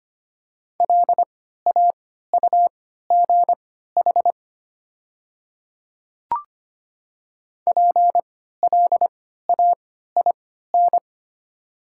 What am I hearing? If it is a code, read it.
Morse code, heard as a keyed tone, about 25 wpm, spaced as for 14 wpm, 700 Hz. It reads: LAUGH PLAIN
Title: 3x Character Spacing - Top 500 Words - 25wpm